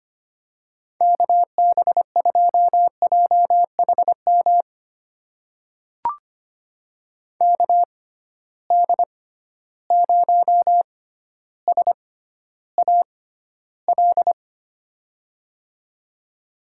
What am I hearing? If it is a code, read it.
Morse code, heard as a keyed tone, about 25 wpm, spaced as for 13 wpm, 700 Hz. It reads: KB2JHM KD0SAL
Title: USA Call Signs - 6 Characters - 6x Character Spacing - 25wpm